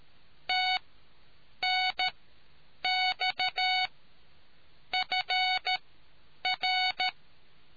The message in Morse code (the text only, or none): TNX FR